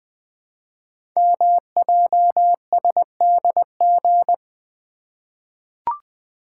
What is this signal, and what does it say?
Morse code, heard as a keyed tone, about 20 wpm, 700 Hz. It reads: MJSDG
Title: Code Groups - 5 Letters - 20wpm